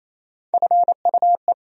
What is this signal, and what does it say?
Morse code, heard as a keyed tone, about 27 wpm, 700 Hz. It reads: FUE
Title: USA Call Sign Postfixes - 3 Letters 28wpm